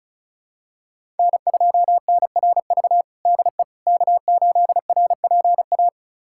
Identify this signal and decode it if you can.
Morse code, heard as a keyed tone, about 34 wpm, 700 Hz. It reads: N2NRV DE K8RPA
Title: USA Callsign DE Callsign - 5 Characters - 35wpm